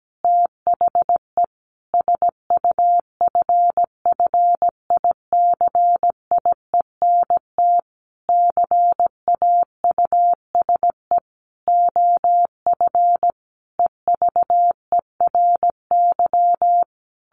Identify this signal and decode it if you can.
Morse code, heard as a keyed tone, about 17 wpm, 700 Hz. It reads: THE SUFFICIENT CAUSE OF EVERY
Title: Sentences from Top 1400 Words 17wpm